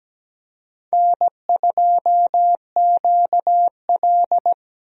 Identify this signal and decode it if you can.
Morse code, heard as a keyed tone, about 17 wpm, 700 Hz. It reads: N2QL
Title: USA Call Signs - 4 Characters - 17wpm